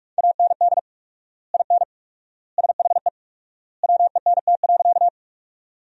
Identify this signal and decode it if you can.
Morse code, heard as a keyed tone, about 46 wpm, 700 Hz. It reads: AND IN SHE WENT.
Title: 4x Word-Spacing - Sentences from Top 200 Words - 45wpm